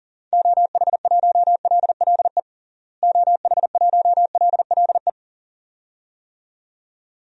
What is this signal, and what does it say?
Morse code, heard as a keyed tone, about 40 wpm, 700 Hz. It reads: OH1LLE OH1LLE
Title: DXCC-100 Call Signs - 6 Characters - Repeated 2x - 40wpm